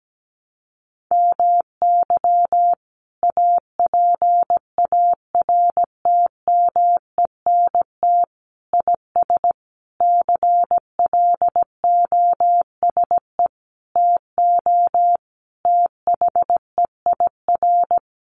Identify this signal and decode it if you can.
Morse code, heard as a keyed tone, about 17 wpm, 700 Hz. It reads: MY APARTMENT IS CLOSE TO THEIR
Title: Sentences from Top 900 Words 17wpm